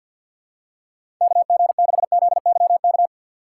Morse code, heard as a keyed tone, about 49 wpm, 700 Hz. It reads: KG6ZYK
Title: USA Call Signs - 4x Character Spacing - 50wpm